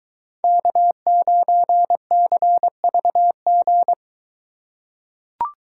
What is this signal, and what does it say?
Morse code, heard as a keyed tone, about 23 wpm, 700 Hz. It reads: K9CVG